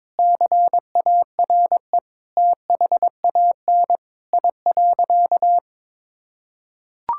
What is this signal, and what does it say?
Morse code, heard as a keyed tone, about 22 wpm, 700 Hz. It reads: CARE THAN I.
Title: Sentences from Top 1000 Words - Review Part 5 - 22wpm